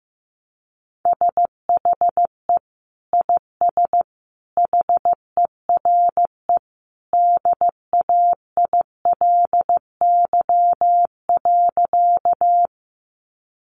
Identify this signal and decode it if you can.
Morse code, heard as a keyed tone, about 15 wpm, 700 Hz. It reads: SHE IS HERE DAILY.
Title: Sentences from Top 1100 Words 15wpm